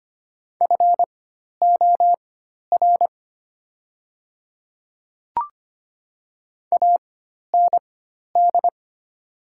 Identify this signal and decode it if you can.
Morse code, heard as a keyed tone, about 25 wpm, spaced as for 12 wpm, 700 Hz. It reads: FOR AND
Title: 4x Character Spacing - Top 100 Words - 25wpm